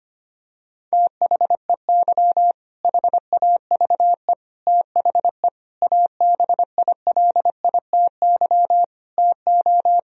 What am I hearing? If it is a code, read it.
Morse code, heard as a keyed tone, about 25 wpm, 700 Hz. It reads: THEY HAVE THE ABILITY TO
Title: Sentences from Top 1000 Words - Review Part 3 - 25wpm